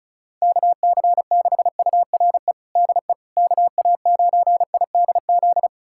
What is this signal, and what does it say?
Morse code, heard as a keyed tone, about 36 wpm, 700 Hz. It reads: KC6URE DE KA9IDZ
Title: USA Callsign DE Callsign - 6 Characters - 35wpm